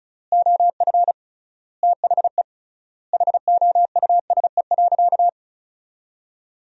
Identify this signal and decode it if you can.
Morse code, heard as a keyed tone, about 34 wpm, 700 Hz. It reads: OF THE HOUSE.
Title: 3x Word-Spacing - Sentences from Top 400 Words - 35wpm